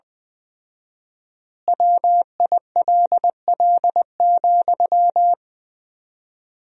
Morse code, heard as a keyed tone, about 20 wpm, 700 Hz. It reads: WILL,
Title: 4x Word-Spacing - Sentences from Top 200 Words - 20wpm